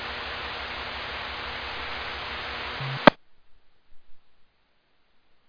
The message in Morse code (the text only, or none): none